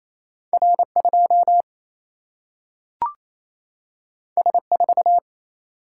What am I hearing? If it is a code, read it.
Morse code, heard as a keyed tone, about 28 wpm, 700 Hz. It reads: R2 S4
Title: Single-Letter Single-Number 28wpm